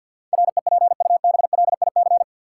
Morse code, heard as a keyed tone, about 53 wpm, 700 Hz. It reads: REPUBLIC
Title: Sentences from Top 1500 Words - New Word Drill - 50wpm